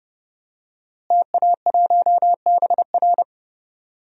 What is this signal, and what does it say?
Morse code, heard as a keyed tone, about 32 wpm, 700 Hz. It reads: TA1BR
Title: DXCC-100 Call Signs - 5 Characters - Repeated 2x - 30wpm